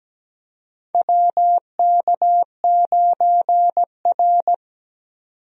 Morse code, heard as a keyed tone, about 17 wpm, 700 Hz. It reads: WK9R
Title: USA Call Signs - 4 Characters - 17wpm